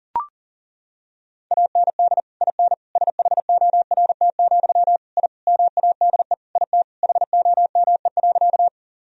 Morse code, heard as a keyed tone, about 40 wpm, 700 Hz. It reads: AND IN SHORT, I MADE IT HOME.
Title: Sentences from Top 1000 Words - Review Part 2 - 40wpm